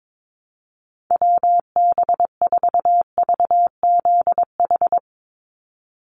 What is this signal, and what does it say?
Morse code, heard as a keyed tone, about 22 wpm, 700 Hz. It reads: WB4VZH
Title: USA Call Signs 22wpm